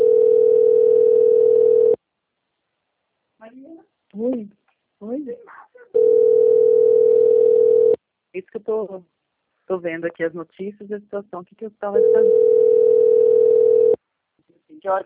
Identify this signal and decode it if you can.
Ringback tone (North America), 440 + 480 Hz, with speech.
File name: rg-1600-unknown- - - .28770